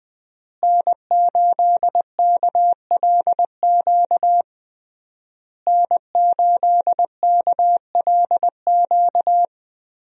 Morse code, heard as a keyed tone, about 20 wpm, 700 Hz. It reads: N8KLQ N8KLQ
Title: USA Call Signs - 5 Characters - Repeated 3x - 20wpm